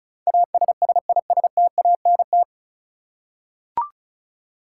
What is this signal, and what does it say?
Morse code, heard as a keyed tone, about 36 wpm, 700 Hz. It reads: ASSISTANT
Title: Practice QSO - Exchange 3 - More Info - 35wpm